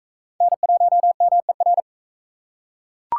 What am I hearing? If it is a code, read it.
Morse code, heard as a keyed tone, about 41 wpm, 700 Hz. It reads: N1MER